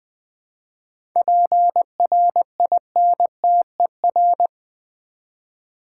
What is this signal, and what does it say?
Morse code, heard as a keyed tone, about 20 wpm, 700 Hz. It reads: PRINTER
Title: Easy Words with Hints - 20wpm